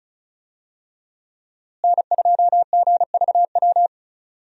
Morse code, heard as a keyed tone, about 36 wpm, 700 Hz. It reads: N2GVW
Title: USA Call Signs - 5 Characters - Repeated 3x - 35wpm